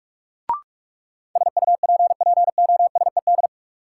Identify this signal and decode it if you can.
Morse code, heard as a keyed tone, about 43 wpm, 700 Hz. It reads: SUPPOSED
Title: Sets of 3 Words - New Word Drill - 45wpm